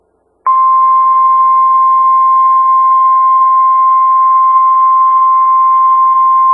Emergency Alert System (SAME) tones (the 1050 Hz attention tone). It is heard from 0.5 to 5.8 s.
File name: Skywarn/2013/2013-03-18-Newton County WA4ASI Repeater